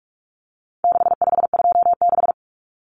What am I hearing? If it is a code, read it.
Morse code, heard as a keyed tone, about 44 wpm, 700 Hz. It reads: 6526
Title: Numbers - 4-Digit 45wpm